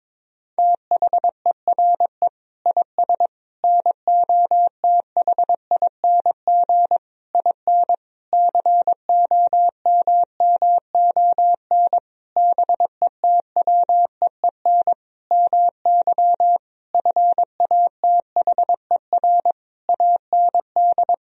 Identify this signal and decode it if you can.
Morse code, heard as a keyed tone, about 22 wpm, 700 Hz. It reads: THERE IS NOTHING IN COMMON BETWEEN MY FATHER AND
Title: Sentences from Top 1000 Words - Review Part 3 - 22wpm